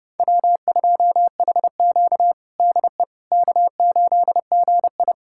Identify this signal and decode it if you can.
Morse code, heard as a keyed tone, about 30 wpm, 700 Hz. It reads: W2HQ DE K8GI
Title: USA Callsign DE Callsign - 4 Characters - 30wpm